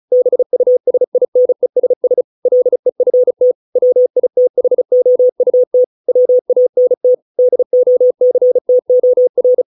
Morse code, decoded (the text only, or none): BUSINESS LEFT WITHOUT WANT DOCTOR